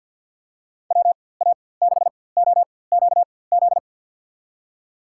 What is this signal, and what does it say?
Morse code, heard as a keyed tone, about 53 wpm, spaced as for 24 wpm, 700 Hz. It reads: WA6OQZ
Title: USA Call Signs - 6 Characters - 4x Character Spacing - 50wpm